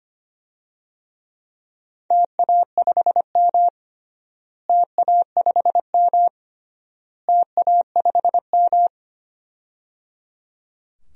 Morse code, heard as a keyed tone, about 25 wpm, 700 Hz. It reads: TA5M TA5M TA5M
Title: DXCC-100 Call Signs - 4 Characters - Repeated 3x - 25wpm